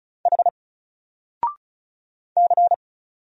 Morse code, heard as a keyed tone, about 33 wpm, 700 Hz. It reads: H C
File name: Single-Letter-Number-35wpm